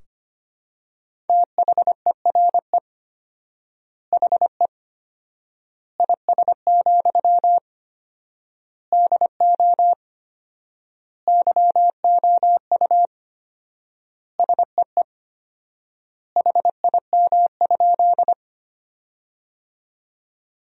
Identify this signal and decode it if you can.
Morse code, heard as a keyed tone, about 25 wpm, 700 Hz. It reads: THERE HE IS, DO YOU SEE HIM?
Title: 4x Word-Spacing - Sentences from Top 100 Words - 25wpm